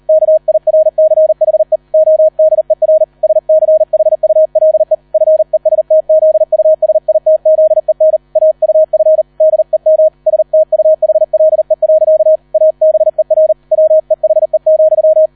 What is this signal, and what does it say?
Morse code, heard as a keyed tone, about 38 wpm, 630 Hz. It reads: KIRCHE ODER SCHULE FESTZUSITZEN AUF DEM STUHLE. ABER WEHE,